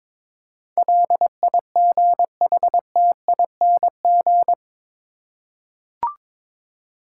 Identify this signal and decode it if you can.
Morse code, heard as a keyed tone, about 22 wpm, 700 Hz. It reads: LIGHTING
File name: practice / 22wpm / Eight Letter Words - 22wpm